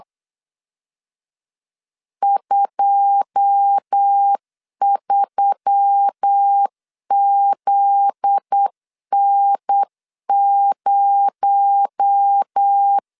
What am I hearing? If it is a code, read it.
Morse code, heard as a keyed tone, about 8 wpm, 800 Hz. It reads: 23ZN0
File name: CW / DET / 1 9 1500C S A 8wpm 5637